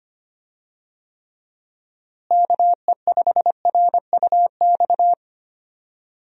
Morse code, heard as a keyed tone, about 25 wpm, 700 Hz. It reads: KE5RUX
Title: USA Call Signs - 6 Characters - Repeated 2x - 25wpm